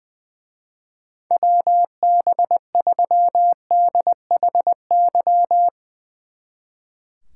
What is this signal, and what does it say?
Morse code, heard as a keyed tone, about 20 wpm, 700 Hz. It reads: WB3DHY
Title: Speed Racing - USA Call Signs 20wpm